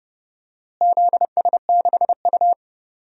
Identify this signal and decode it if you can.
Morse code, heard as a keyed tone, about 30 wpm, 700 Hz. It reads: ZS6U
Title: DXCC-100 Call Signs - 4 Characters - Repeated 3x - 30wpm